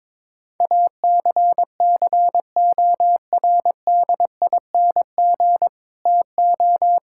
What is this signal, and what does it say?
Morse code, heard as a keyed tone, about 22 wpm, 700 Hz. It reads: ACCORDING TO
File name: Sentences with 7-Word Phrases - 22wpm